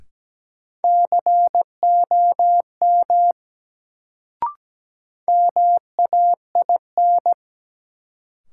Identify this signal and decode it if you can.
Morse code, heard as a keyed tone, about 17 wpm, 700 Hz. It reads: COM MAIN
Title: Sentences from Top 1300 Words - New Word Drill Alt - 17wpm